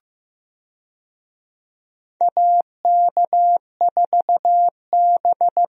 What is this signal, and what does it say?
Morse code, heard as a keyed tone, about 15 wpm, 700 Hz. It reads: AK4B
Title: USA Call Signs - 4 Characters - Repeated 3x - 15wpm